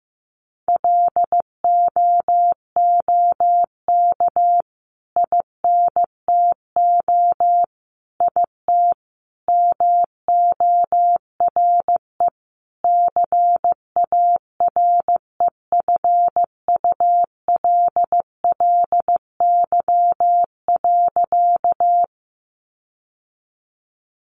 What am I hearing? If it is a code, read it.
Morse code, heard as a keyed tone, about 15 wpm, 700 Hz. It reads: LOOK INTO IT MORE CAREFULLY.
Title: Sentences from Top 1300 Words 15wpm